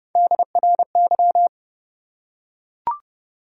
Morse code, heard as a keyed tone, about 30 wpm, 700 Hz. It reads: DRY
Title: Three Letter Words - 30wpm